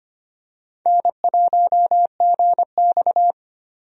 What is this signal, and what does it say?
Morse code, heard as a keyed tone, about 25 wpm, 700 Hz. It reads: N1GX
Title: USA Call Signs - 4 Characters - 25wpm